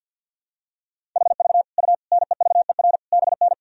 Morse code, heard as a keyed tone, about 46 wpm, 700 Hz. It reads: HV U NEVER BN
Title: Sentences from Top 300 Words and Abbreviations 50wpm